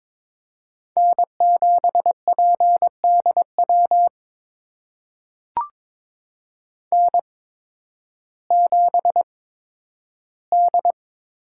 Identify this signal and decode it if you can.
Morse code, heard as a keyed tone, about 22 wpm, spaced as for 12 wpm, 700 Hz. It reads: N7PDW N7D
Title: USA Call Signs - 5 Characters - 8x Character Spacing - 22wpm